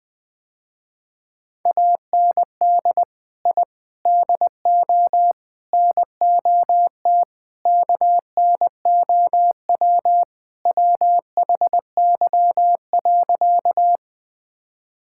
Morse code, heard as a keyed tone, about 20 wpm, 700 Hz. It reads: AND I DO NOT KNOW WHY.
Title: Sentences with 5-Word Phrases - Spoken Beginning - 20wpm